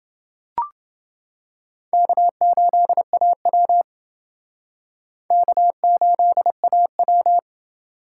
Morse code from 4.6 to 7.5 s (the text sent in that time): K8AW